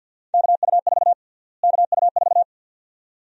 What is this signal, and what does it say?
Morse code, heard as a keyed tone, about 52 wpm, 700 Hz. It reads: KU4 KU4
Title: USA Call Sign Prefixes - Repeated 2x - 50wpm